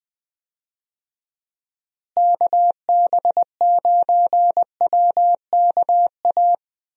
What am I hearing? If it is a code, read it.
Morse code, heard as a keyed tone, about 20 wpm, 700 Hz. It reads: KB9WKA
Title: USA Call Signs - 6 Characters - Repeated 3x - 20wpm